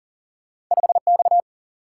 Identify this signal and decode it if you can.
Morse code, heard as a keyed tone, about 39 wpm, 700 Hz. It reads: 5X
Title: Single-Letter Single-Number 40wpm